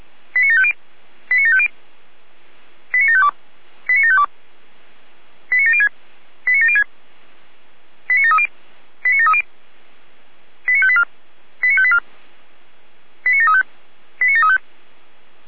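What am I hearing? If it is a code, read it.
Five-tone selective call (ZVEI), heard as quick runs of five tones, tones of about 80 ms. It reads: 89750 89750 89752 89752 89706 89706 89730 89730 89584 89584 89735 89735
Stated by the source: ZVEI five-tone selective call received by SDR.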